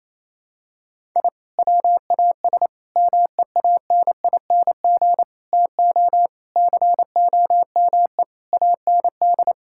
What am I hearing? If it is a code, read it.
Morse code, heard as a keyed tone, about 28 wpm, 700 Hz. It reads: I WAS MEANING TO COME AND
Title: Sentences from Top 1000 Words - Review Part 6 - 28wpm